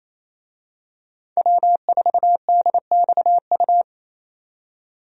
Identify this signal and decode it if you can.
Morse code, heard as a keyed tone, about 28 wpm, 700 Hz. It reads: W4DXU
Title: USA Call Signs - 5 Characters - 28wpm